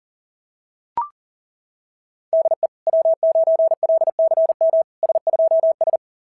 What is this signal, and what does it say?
Morse code, heard as a keyed tone, about 37 wpm, 660 Hz. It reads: DE W9LCM S2S
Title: SOTA QSO Practice 40wpm